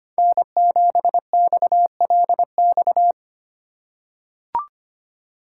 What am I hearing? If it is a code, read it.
Morse code, heard as a keyed tone, about 25 wpm, 700 Hz. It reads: N7XLX
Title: USA Call Signs - 2x Character Spacing - 25wpm